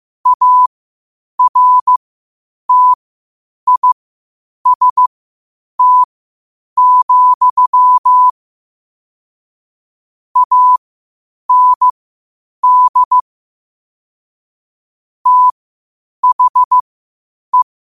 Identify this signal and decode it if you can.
Morse code, heard as a keyed tone, about 15 wpm, spaced as for 8 wpm, 1000 Hz. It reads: ARTIST, AND THE
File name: qotdfiles / 8wpm